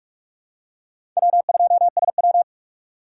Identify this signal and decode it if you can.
Morse code, heard as a keyed tone, about 41 wpm, 700 Hz. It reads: W2SW
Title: USA Call Signs - 4 Characters - 4x Character Spacing - 45wpm